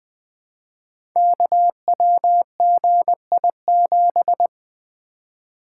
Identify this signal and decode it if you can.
Morse code, heard as a keyed tone, about 20 wpm, 700 Hz. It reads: KWGI7
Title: Code Groups - 5 Letter:Numbers - 20wpm